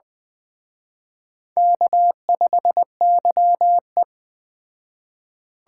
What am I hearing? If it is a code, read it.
Morse code, heard as a keyed tone, about 20 wpm, 700 Hz. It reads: K5YE
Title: Speed Racing - USA 1x2 Call Signs 17wpm